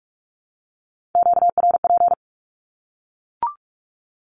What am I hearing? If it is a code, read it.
Morse code, heard as a keyed tone, about 44 wpm, 700 Hz. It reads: QRP
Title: Q-Codes 45wpm